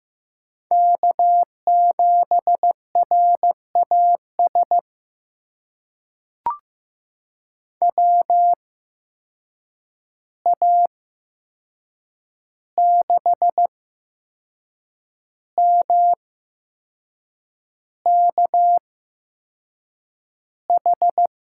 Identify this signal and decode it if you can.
Morse code, heard as a keyed tone, about 15 wpm, spaced as for 6 wpm, 700 Hz. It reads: K7RAS WA6MKH
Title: USA Call Signs - 8x Character Spacing - 15wpm